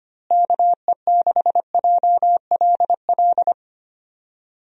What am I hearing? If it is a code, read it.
Morse code, heard as a keyed tone, about 25 wpm, 700 Hz. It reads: KE6JLL
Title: USA Call Signs - 6 Characters - Repeated 3x - 25wpm